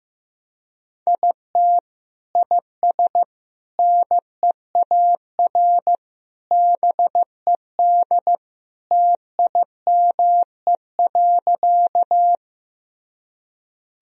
Morse code, heard as a keyed tone, about 15 wpm, 700 Hz. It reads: IT IS NEAR BED TIME.
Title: Sentences from Top 1000 Words - Review Part 1 - 15wpm